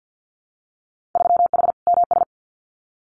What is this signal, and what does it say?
Morse code, heard as a keyed tone, about 50 wpm, 700 Hz. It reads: 3H MS
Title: Winter Field Day Exchanges Encore 50wpm